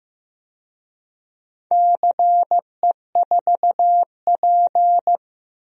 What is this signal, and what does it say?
Morse code, heard as a keyed tone, about 16 wpm, 700 Hz. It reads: CE4P
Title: DXCC-100 Call Signs - 4 Characters - Repeated 2x - 15wpm